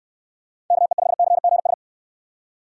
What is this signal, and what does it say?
Morse code, heard as a keyed tone, about 62 wpm, 700 Hz. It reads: KE5CGS